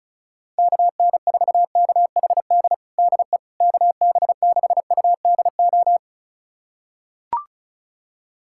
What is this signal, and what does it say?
Morse code, heard as a keyed tone, about 36 wpm, 700 Hz. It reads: KN4KHD DE KB6UDO
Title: USA Callsign DE Callsign 35wpm